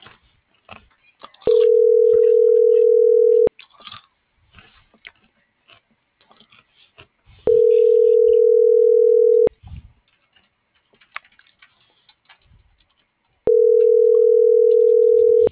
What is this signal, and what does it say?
Ringback tone (North America), 440 + 480 Hz.